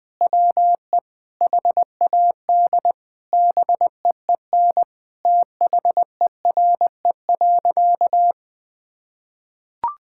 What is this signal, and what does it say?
Morse code, heard as a keyed tone, about 20 wpm, 700 Hz. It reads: WE HAD BEEN THERE.
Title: Sentences with 4-Word Phrases - 20wpm